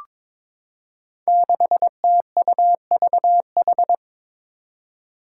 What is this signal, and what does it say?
Morse code, heard as a keyed tone, about 22 wpm, 700 Hz. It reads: 6TUVH
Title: Five Challenging Letter-Number Combinations 22wpm